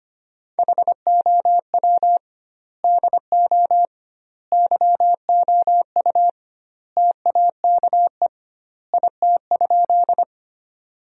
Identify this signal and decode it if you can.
Morse code, heard as a keyed tone, about 25 wpm, 700 Hz. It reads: HOW DO YOU TAKE IT?
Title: 2x Word-Spacing - Sentences from Top 200 Words - 25wpm